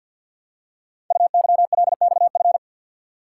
Morse code, heard as a keyed tone, about 48 wpm, 700 Hz. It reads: UYLXF
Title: Code Groups - 5 Letters - 50wpm